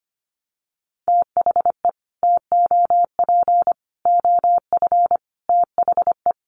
Morse code, heard as a keyed tone, about 25 wpm, 700 Hz. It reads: THE TOP OF THE
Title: Sets of 5 Words Encore 25wpm